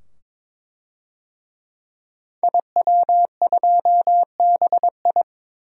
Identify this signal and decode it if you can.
Morse code, heard as a keyed tone, about 22 wpm, 700 Hz. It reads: IW2BI
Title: DXCC-100 Call Signs - 5 Characters - 22wpm